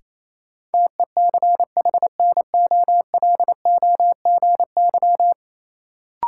Morse code, heard as a keyed tone, about 28 wpm, 700 Hz. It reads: TECHNOLOGY
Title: Sentences from Top 1000 Words - Review Part 5 - New Word Drill - 28wpm